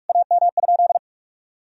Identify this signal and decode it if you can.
Morse code, heard as a keyed tone, about 46 wpm, 700 Hz. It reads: AM?
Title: Sentences from Top 1000 Words - Review Part 2 - 45wpm